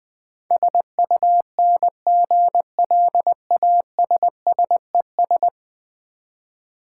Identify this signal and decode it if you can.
Morse code, heard as a keyed tone, about 20 wpm, 700 Hz. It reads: SUNGLASSES
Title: Easy Words with Hints - 20wpm